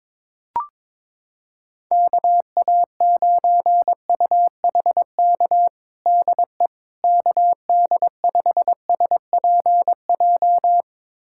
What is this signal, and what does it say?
Morse code, heard as a keyed tone, about 21 wpm, 700 Hz. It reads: KA9UHK DE KD5SPJ
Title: USA Callsign DE Callsign - 6 Characters - 22wpm